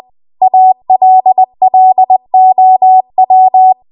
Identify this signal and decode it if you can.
Morse code, heard as a keyed tone, about 20 wpm, 750 Hz. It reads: ALLOW